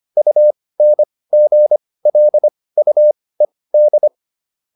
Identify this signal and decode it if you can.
Morse code, heard as a keyed tone, about 25 wpm, spaced as for 18 wpm, 600 Hz. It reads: UNGLUED